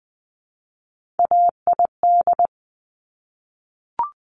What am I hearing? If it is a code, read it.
Morse code, heard as a keyed tone, about 20 wpm, 700 Hz. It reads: AID
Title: Top Three Letter Combinations in Words 20wpm